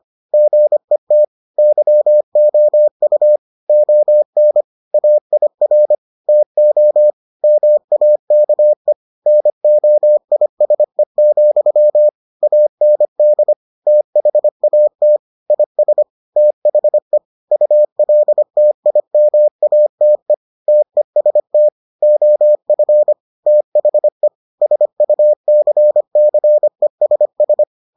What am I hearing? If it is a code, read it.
Morse code, heard as a keyed tone, about 25 wpm, 600 Hz. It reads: GET YOU ON AIR TO MAKE NOISE, AND THAT IS THE ULTIMATE TEST OF THE SUCCESS